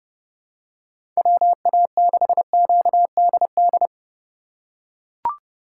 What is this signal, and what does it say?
Morse code, heard as a keyed tone, about 30 wpm, 700 Hz. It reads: WA6QDD
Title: USA Call Signs - 6 Characters - 4x Character Spacing - 30wpm